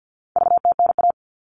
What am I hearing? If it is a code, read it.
Morse code, heard as a keyed tone, about 48 wpm, 700 Hz. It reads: VTNA
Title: All Cut Numbers - 4-Digit 50wpm